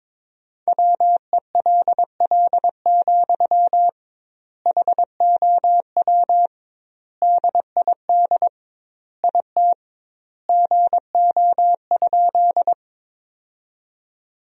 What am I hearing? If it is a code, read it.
Morse code, heard as a keyed tone, about 22 wpm, 700 Hz. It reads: WELL, HOW DID IT GO?
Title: 2x Word-Spacing - Sentences from Top 200 Words - 22wpm